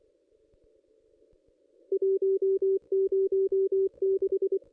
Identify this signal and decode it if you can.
Morse code, heard as a keyed tone, about 24 wpm, 390 Hz. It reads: 106
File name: contests / ss-cw-k5tr-op-n5rz -1800